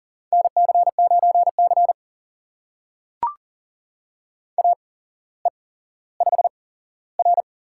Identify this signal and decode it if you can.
Morse code, heard as a keyed tone, about 38 wpm, spaced as for 17 wpm, 690 Hz. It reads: NC9C AE5R
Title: USA Call Signs - 4 Characters - 8x Character Spacing - 40wpm